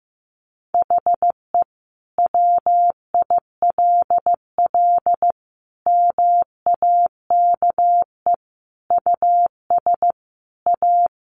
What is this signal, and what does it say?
Morse code, heard as a keyed tone, about 15 wpm, 700 Hz. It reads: HE WILL MAKE US A
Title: Sentences from Top 1200 Words 15wpm